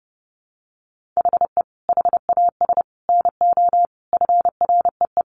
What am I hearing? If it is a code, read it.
Morse code, heard as a keyed tone, about 30 wpm, 700 Hz. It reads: HE HAS NO FREE
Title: Sentences from Top 1400 Words 30wpm